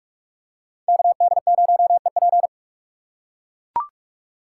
Morse code, heard as a keyed tone, about 43 wpm, 700 Hz. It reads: KD0EP